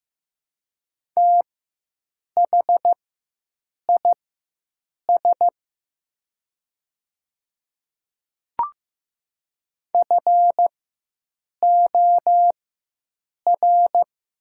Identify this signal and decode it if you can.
Morse code, heard as a keyed tone, about 15 wpm, spaced as for 7 wpm, 700 Hz. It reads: THIS FOR
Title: 4x Character Spacing - Top 200 Words - 15wpm